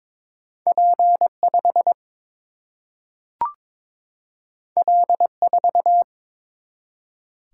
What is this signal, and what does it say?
Morse code, heard as a keyed tone, about 22 wpm, 700 Hz. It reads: P 5 L 4
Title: Single-Letter Single-Number 22wpm